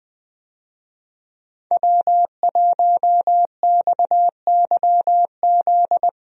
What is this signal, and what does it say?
Morse code, heard as a keyed tone, about 20 wpm, 700 Hz. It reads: W1XYZ
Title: USA Call Signs - 5 Characters - 20wpm